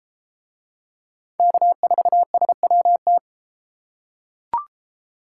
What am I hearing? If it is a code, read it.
Morse code, heard as a keyed tone, about 31 wpm, 700 Hz. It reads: K4SWT